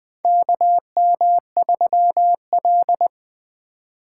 Morse code, heard as a keyed tone, about 21 wpm, 700 Hz. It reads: KM3L
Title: USA Call Signs - 4 Characters - 3x Character Spacing - 20wpm